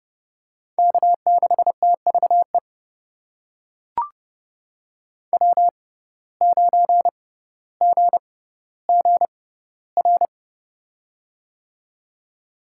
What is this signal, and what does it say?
Morse code, heard as a keyed tone, about 32 wpm, spaced as for 16 wpm, 700 Hz. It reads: K6TVE W9GGR
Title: USA Call Signs - 5 Characters - 6x Character Spacing - 30wpm